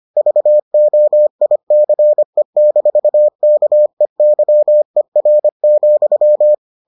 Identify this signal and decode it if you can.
Morse code, heard as a keyed tone, about 25 wpm, 600 Hz. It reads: VOICE-KEYER,